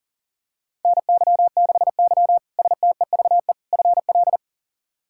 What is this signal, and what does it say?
Morse code, heard as a keyed tone, about 40 wpm, 700 Hz. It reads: NY6Y STEVE FL
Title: K1USN SST QSO Practice - 40wpm